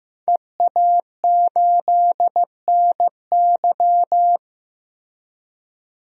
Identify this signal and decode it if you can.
Morse code, heard as a keyed tone, about 15 wpm, 700 Hz. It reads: EA8NY
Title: DXCC-100 Call Signs - 5 Characters - Repeated 2x - 15wpm